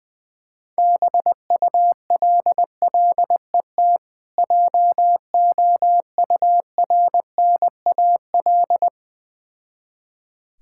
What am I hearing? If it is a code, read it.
Morse code, heard as a keyed tone, about 20 wpm, 700 Hz. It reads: BULLET JOURNAL
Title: Difficult Words with Hints - 20wpm